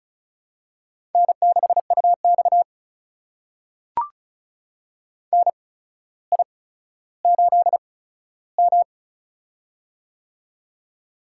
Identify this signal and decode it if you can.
Morse code, heard as a keyed tone, about 36 wpm, spaced as for 15 wpm, 700 Hz. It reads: N6UX NI8M
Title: USA Call Signs - 4 Characters - 8x Character Spacing - 35wpm